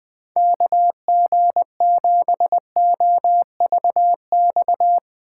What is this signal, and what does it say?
Morse code, heard as a keyed tone, about 22 wpm, 700 Hz. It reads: KG7OVX
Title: Speed Racing - USA 2x3 Call Signs 20wpm